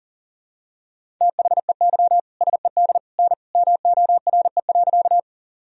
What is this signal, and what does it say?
Morse code, heard as a keyed tone, about 40 wpm, 700 Hz. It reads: THEY SED N MORE.
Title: Sentences from Top 100 Words and Abbreviations 40wpm